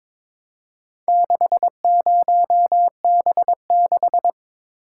Morse code, heard as a keyed tone, about 22 wpm, 690 Hz. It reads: 60B6
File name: practice / 22wpm / Four Letter-Number Combinations - 22wpm